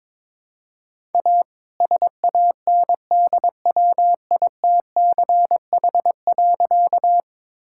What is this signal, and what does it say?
Morse code, heard as a keyed tone, about 22 wpm, 700 Hz. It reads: A SANDWITCH.
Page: morse-code-ninja-practice-22wpm